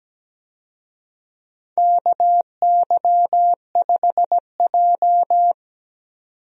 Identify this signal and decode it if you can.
Morse code, heard as a keyed tone, about 17 wpm, 700 Hz. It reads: KY5J
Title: USA Call Signs - 4 Characters - 17wpm